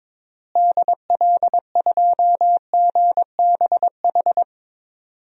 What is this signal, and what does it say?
Morse code, heard as a keyed tone, about 22 wpm, 700 Hz. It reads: DL2GBH
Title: DXCC-100 Call Signs - 6 Characters - Repeated 3x - 22wpm